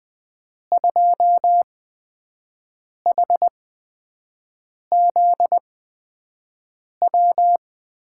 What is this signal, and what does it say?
Morse code, heard as a keyed tone, about 19 wpm, spaced as for 5 wpm, 700 Hz. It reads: 2HZW
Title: USA Call Signs - 6 Characters - 8x Character Spacing - 20wpm